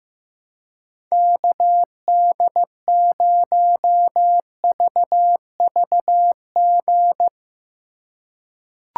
Morse code, heard as a keyed tone, about 15 wpm, 700 Hz. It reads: KD0VVG
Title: USA Call Signs - 6 Characters - 3x Character Spacing - 15wpm